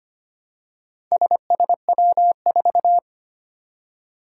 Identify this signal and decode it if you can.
Morse code, heard as a keyed tone, about 25 wpm, 700 Hz. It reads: SSW4
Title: Four Challenging Letter-Number Combinations 25wpm